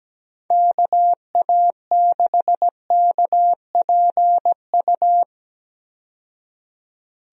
Morse code, heard as a keyed tone, about 17 wpm, 700 Hz. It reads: KA6KPU
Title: USA Call Signs - 6 Characters - 17wpm